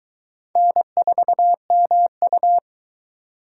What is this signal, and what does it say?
Morse code, heard as a keyed tone, about 23 wpm, 700 Hz. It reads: N4MU